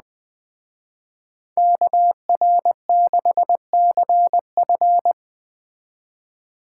Morse code, heard as a keyed tone, about 20 wpm, 700 Hz. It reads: KR6CF
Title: Speed Racing - USA 2x2 Call Signs 17wpm